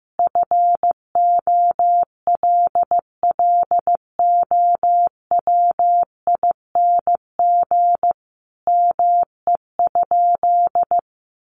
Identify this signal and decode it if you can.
Morse code, heard as a keyed tone, about 15 wpm, 700 Hz. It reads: FOLLOWING ME?
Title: Sentences from Top 700 Words 15wpm